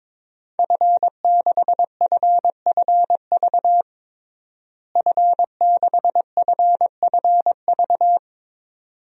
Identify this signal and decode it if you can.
Morse code, heard as a keyed tone, about 22 wpm, 700 Hz. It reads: F6FFV F6FFV
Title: DXCC-100 Call Signs - 5 Characters - Repeated 2x - 22wpm